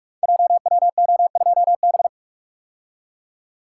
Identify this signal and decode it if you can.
Morse code, heard as a keyed tone, about 44 wpm, 690 Hz. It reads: JWO2B
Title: Code Groups - 5 Letter:Numbers - 45wpm